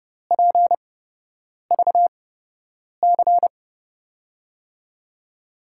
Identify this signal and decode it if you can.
Morse code, heard as a keyed tone, about 31 wpm, spaced as for 8 wpm, 700 Hz. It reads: PVC
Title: USA Call Signs - 8x Character Spacing - 30wpm